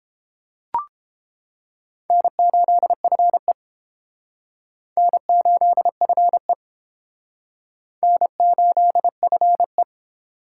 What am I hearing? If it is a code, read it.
Morse code, heard as a keyed tone, about 31 wpm, 700 Hz. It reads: N8FE N8FE N8FE